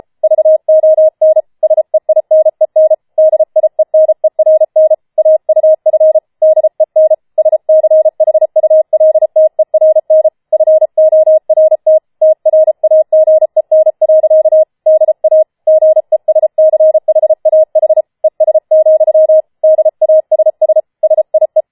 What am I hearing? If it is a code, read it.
Morse code, heard as a keyed tone, about 33 wpm, 630 Hz. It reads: VON SEINEN DIENERN AUF DEN SCHULTERN FORT TRAGEN. DA GESCHAH ES, DASS SIE